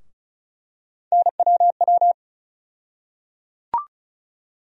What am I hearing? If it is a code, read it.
Morse code, heard as a keyed tone, about 36 wpm, 700 Hz. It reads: NWW
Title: USA Call Sign Postfixes - 3 Letters 35wpm